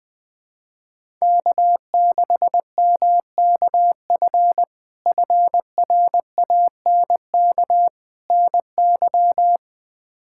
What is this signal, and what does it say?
Morse code, heard as a keyed tone, about 20 wpm, 700 Hz. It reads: K6MKF FRANK NY
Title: K1USN SST QSO Practice - 20wpm